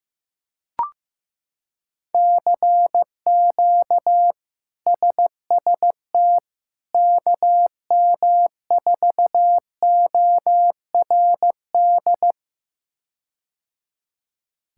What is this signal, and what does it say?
Morse code, heard as a keyed tone, about 15 wpm, 700 Hz. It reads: CQ SST KM4ORD
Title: K1USN SST QSO Practice - 15wpm